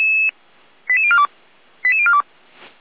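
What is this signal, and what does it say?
Five-tone selective call (ZVEI), heard as quick runs of five tones, tones of about 80 ms. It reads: 88042 88042
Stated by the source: ZVEI five-tone selective call received on an SDR.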